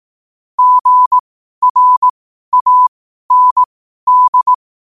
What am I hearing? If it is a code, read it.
Morse code, heard as a keyed tone, about 18 wpm, spaced as for 12 wpm, 1000 Hz. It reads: GRAND
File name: qotdfiles / 12wpm